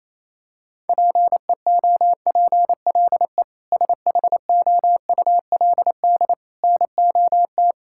Morse code, heard as a keyed tone, about 28 wpm, 700 Hz. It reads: PEOPLE SHOULD NOT
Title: Sentences from Top 1000 Words - Review Part 6 - 28wpm